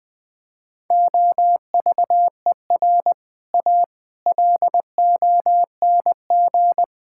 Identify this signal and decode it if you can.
Morse code, heard as a keyed tone, about 20 wpm, 700 Hz. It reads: OVER A LONG
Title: Sentences with 6-Word Phrases - Spoken Beginning - 20wpm